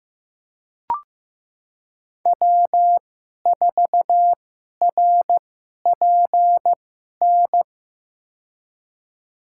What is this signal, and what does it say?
Morse code, heard as a keyed tone, about 15 wpm, spaced as for 11 wpm, 700 Hz. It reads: W4RPN
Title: USA Call Signs - 2x Character Spacing - 15wpm